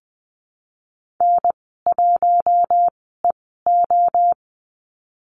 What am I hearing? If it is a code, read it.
Morse code, heard as a keyed tone, about 20 wpm, spaced as for 15 wpm, 700 Hz. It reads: N1EO
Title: 042 Lesson 8g - Number 1 - Call Signs 20wpm